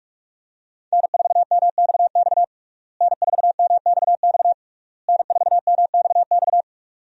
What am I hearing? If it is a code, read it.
Morse code, heard as a keyed tone, about 46 wpm, 700 Hz. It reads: N4MXX N4MXX N4MXX
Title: USA Call Signs - 5 Characters - Repeated 3x - 45wpm